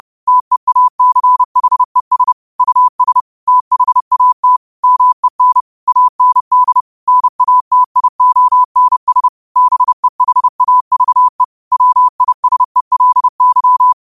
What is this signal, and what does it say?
Morse code, heard as a keyed tone, about 30 wpm, 1000 Hz. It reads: TEACHES US THAT MEN AND NATIONS BEHAVE WISELY
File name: qotdfiles / 30wpm